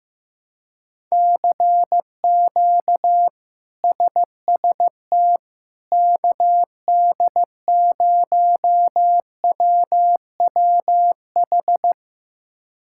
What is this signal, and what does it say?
Morse code, heard as a keyed tone, about 15 wpm, 700 Hz. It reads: CQ SST KD0WWH
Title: K1USN SST QSO Practice - 15wpm